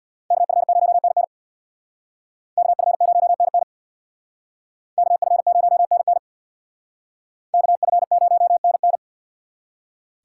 Morse code, heard as a keyed tone, about 64 wpm, 700 Hz. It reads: KF0NN KF0NN KF0NN KF0NN